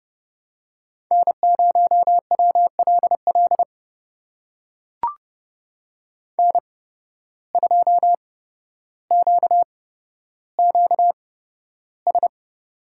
Morse code, heard as a keyed tone, about 31 wpm, spaced as for 13 wpm, 700 Hz. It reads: N0WLL N2QQS
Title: USA Call Signs - 5 Characters - 8x Character Spacing - 30wpm